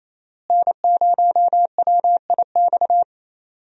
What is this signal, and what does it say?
Morse code, heard as a keyed tone, about 27 wpm, 700 Hz. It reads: N0WIX
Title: USA Call Signs - 5 Characters - Repeated 3x - 28wpm